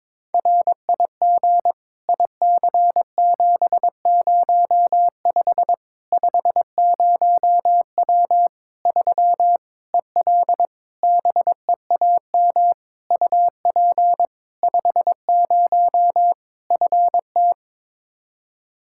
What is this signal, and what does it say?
Morse code, heard as a keyed tone, about 22 wpm, 700 Hz. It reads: RIG IC705 50W 3 EL BEAM UP 50 FT
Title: Practice QSO - Exchange 2 - Info - 22wpm